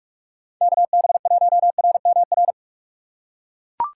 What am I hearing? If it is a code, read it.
Morse code, heard as a keyed tone, about 41 wpm, 700 Hz. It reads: KB1RMR